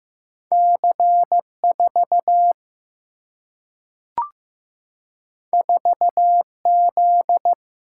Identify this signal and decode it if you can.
Morse code, heard as a keyed tone, about 15 wpm, 700 Hz. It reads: C4 4Z
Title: Single-Letter Single-Number 15wpm